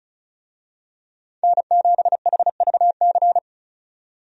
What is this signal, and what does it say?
Morse code, heard as a keyed tone, about 34 wpm, 700 Hz. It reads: N7HVC
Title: USA Call Signs - 5 Characters - 35wpm